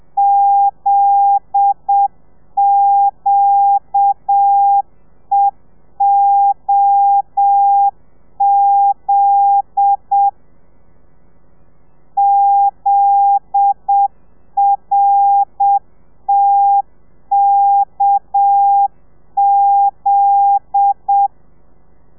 Morse code, heard as a keyed tone, about 7 wpm, 800 Hz. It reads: ZQEOZ ZRTKZ